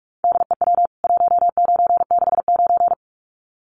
Morse code, heard as a keyed tone, about 44 wpm, 700 Hz. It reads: DEW 1969
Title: CWT Exchange 45wpm